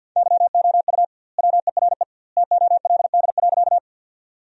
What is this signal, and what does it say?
Morse code, heard as a keyed tone, about 50 wpm, 700 Hz. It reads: YOU WERE TOLD.
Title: 2x Word-Spacing - Sentences from Top 400 Words - 50wpm